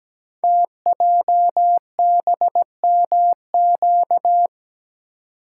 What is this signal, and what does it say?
Morse code, heard as a keyed tone, about 17 wpm, 700 Hz. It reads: TJBMQ
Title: Code Groups - 5 Letters - 17wpm